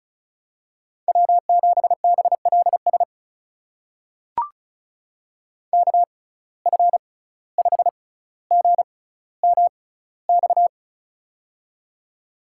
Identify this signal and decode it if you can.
Morse code, heard as a keyed tone, about 36 wpm, spaced as for 17 wpm, 700 Hz. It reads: W7BLS KF5GMX
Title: USA Call Signs - 6x Character Spacing - 35wpm